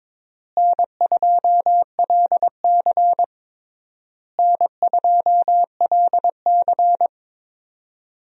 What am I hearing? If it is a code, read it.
Morse code, heard as a keyed tone, about 22 wpm, 700 Hz. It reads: N2LC N2LC
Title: USA Call Signs - 4 Characters - Repeated 3x - 22wpm